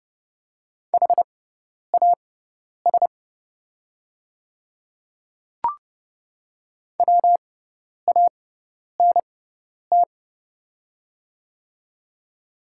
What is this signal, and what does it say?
Morse code, heard as a keyed tone, about 30 wpm, spaced as for 10 wpm, 700 Hz. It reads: HAS WANT
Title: 6x Character Spacing - Top 200 Words - 30wpm